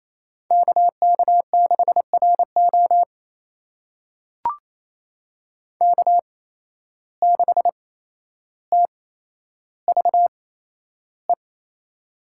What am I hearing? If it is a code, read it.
Morse code, heard as a keyed tone, about 28 wpm, spaced as for 12 wpm, 700 Hz. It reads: KK6RO K6TVE
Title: USA Call Signs - 5 Characters - 8x Character Spacing - 28wpm